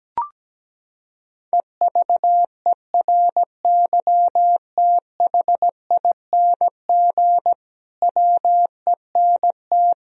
Morse code, heard as a keyed tone, about 17 wpm, 700 Hz. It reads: EVERYTHING WENT
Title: Sentences from Top 1000 Words - Review Part 4 - 17wpm